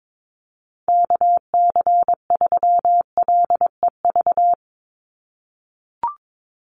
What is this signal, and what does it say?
Morse code, heard as a keyed tone, about 22 wpm, 700 Hz. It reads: KC3LEV
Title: USA Call Signs 22wpm